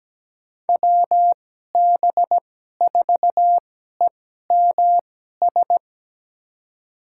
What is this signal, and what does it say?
Morse code, heard as a keyed tone, about 17 wpm, spaced as for 12 wpm, 700 Hz. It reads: WB4EMS
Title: USA Call Signs - 2x Character Spacing - 17wpm